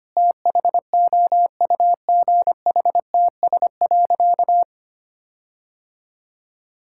Morse code, heard as a keyed tone, about 25 wpm, 700 Hz. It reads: THOUGHTS.
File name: practice / 25wpm / All-Age Appropriate Jokes - 25wpm